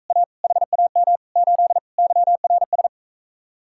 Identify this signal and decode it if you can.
Morse code, heard as a keyed tone, about 45 wpm, 700 Hz. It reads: A HAM 8 YRS